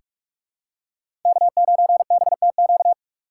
Morse code, heard as a keyed tone, about 46 wpm, 700 Hz. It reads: K9BTQ
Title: K1USN SST QSO Practice - 45wpm